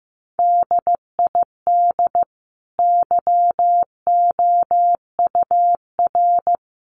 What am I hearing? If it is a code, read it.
Morse code, heard as a keyed tone, about 15 wpm, 700 Hz. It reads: DID YOUR
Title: Sentences from Top 1100 Words 15wpm